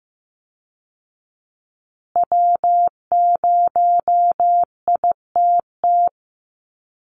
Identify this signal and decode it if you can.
Morse code, heard as a keyed tone, about 15 wpm, 700 Hz. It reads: W0ITT
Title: Call Signs Repeated 2x - 15wpm